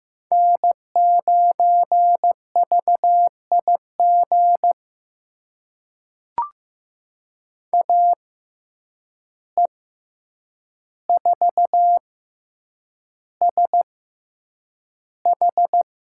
Morse code, heard as a keyed tone, about 15 wpm, spaced as for 8 wpm, 700 Hz. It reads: N9VIG AE4SH
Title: USA Call Signs - 5 Characters - 6x Character Spacing - 15wpm